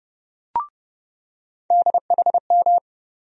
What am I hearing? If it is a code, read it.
Morse code, heard as a keyed tone, about 30 wpm, 700 Hz. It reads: DHM
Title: USA Call Sign Postfixes - 3 Letters 30wpm